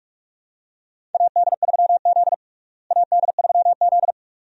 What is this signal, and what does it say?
Morse code, heard as a keyed tone, about 42 wpm, 700 Hz. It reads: AD3Z AD3Z
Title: USA Call Signs - 4 Characters - Repeated 2x - 45wpm